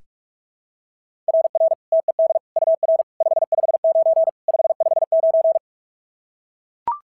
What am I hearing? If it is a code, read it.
Morse code, heard as a keyed tone, about 42 wpm, 690 Hz. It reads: RR TED UR 559 559
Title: SOTA QSO Practice 45wpm